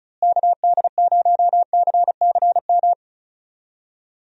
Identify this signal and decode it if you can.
Morse code, heard as a keyed tone, about 34 wpm, 700 Hz. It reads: KD0CCM
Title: USA Call Signs - 6 Characters - Repeated 2x - 35wpm